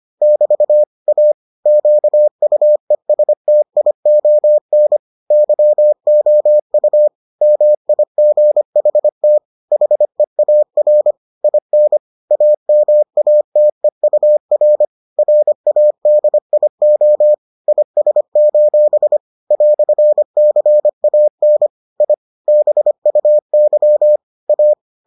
Morse code, heard as a keyed tone, about 25 wpm, 600 Hz. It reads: <BT> A QUESTION YOU MIGHT HEAR IN AMATEUR RADIO IS: "CAN I BUY A